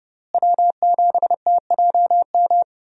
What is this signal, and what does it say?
Morse code, heard as a keyed tone, about 30 wpm, 700 Hz. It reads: W7TJM
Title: USA Call Signs - 5 Characters - Repeated 3x - 30wpm